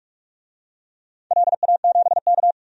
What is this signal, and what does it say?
Morse code, heard as a keyed tone, about 44 wpm, 700 Hz. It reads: LA7K
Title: DXCC-100 Call Signs - 4 Characters - 45wpm